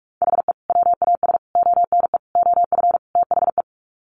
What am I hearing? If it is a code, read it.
Morse code, heard as a keyed tone, about 44 wpm, 700 Hz. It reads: HE WAS ONE OF THE
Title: Sets of 5 Words 45wpm